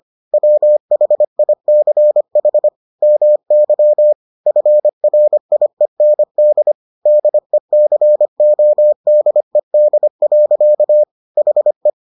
Morse code, heard as a keyed tone, about 25 wpm, 600 Hz. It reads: WHICH MY FRIEND DECODED. HE